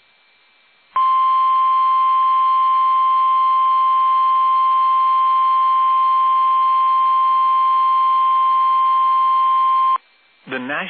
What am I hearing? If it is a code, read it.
Emergency Alert System (SAME) tones (the 1050 Hz attention tone), with speech. It is heard from 0.9 to 10.0 s.